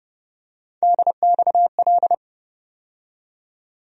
Morse code, heard as a keyed tone, about 30 wpm, 700 Hz. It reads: DXL
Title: USA Call Sign Postfixes - 3 Letters 30wpm